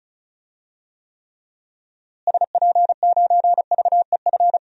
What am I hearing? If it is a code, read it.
Morse code, heard as a keyed tone, about 34 wpm, 700 Hz. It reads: SP9VEF
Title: DXCC-100 Call Signs - 6 Characters - Repeated 2x - 35wpm